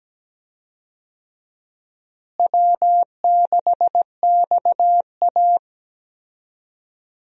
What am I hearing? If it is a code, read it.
Morse code, heard as a keyed tone, about 17 wpm, 700 Hz. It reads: W6XA
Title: USA Call Signs - 4 Characters - Repeated 3x - 17wpm